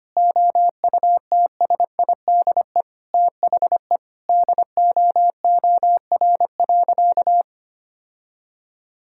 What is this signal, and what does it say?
Morse code, heard as a keyed tone, about 25 wpm, 700 Hz. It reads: OUTSIDE THE DOOR.
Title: Sentences from Top 1000 Words - Review Part 6 - 25wpm